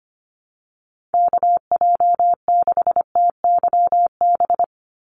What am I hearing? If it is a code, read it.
Morse code, heard as a keyed tone, about 25 wpm, 700 Hz. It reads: KJ6TYB
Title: USA Call Signs 25wpm